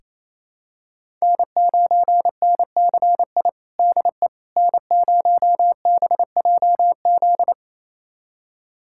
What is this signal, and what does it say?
Morse code, heard as a keyed tone, about 27 wpm, 700 Hz. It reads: N9NCI DE N0BJZ
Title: USA Callsign DE Callsign - 5 Characters - 28wpm